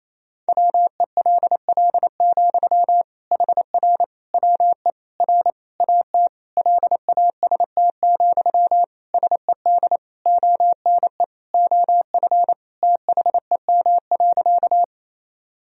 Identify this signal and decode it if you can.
Morse code, heard as a keyed tone, about 28 wpm, 700 Hz. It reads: WELL, HR WE R AT LAST, SED ONE OF THEM.
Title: Sentences from Top 300 Words and Abbreviations 28wpm